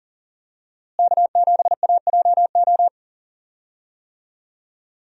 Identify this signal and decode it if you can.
Morse code, heard as a keyed tone, about 40 wpm, 700 Hz. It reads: K7AJO
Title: USA Call Signs - 5 Characters - Repeated 3x - 40wpm